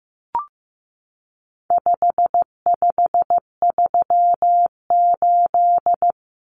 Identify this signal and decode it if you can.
Morse code, heard as a keyed tone, about 15 wpm, 700 Hz. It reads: 5538
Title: Common Cut Numbers - 4-Digit 15wpm